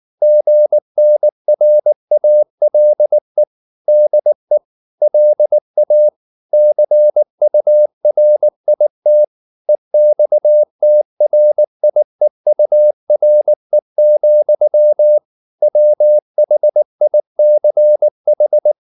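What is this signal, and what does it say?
Morse code, heard as a keyed tone, about 19 wpm, 600 Hz. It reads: GNRALE DE LA CURIT EXTRIEURE, WHICH
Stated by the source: Morse code at 19 wpm from CW med Gnister, reading Ken Follett's novel Never in English.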